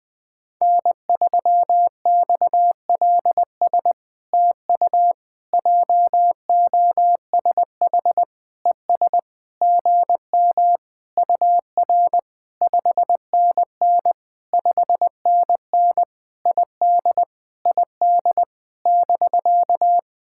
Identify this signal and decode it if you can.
Morse code, heard as a keyed tone, about 19 wpm, 700 Hz. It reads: N3XLS TU JOSH ES GM UR 5NN 5NN ID ID <BK>
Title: POTA QSO Practice - 20wpm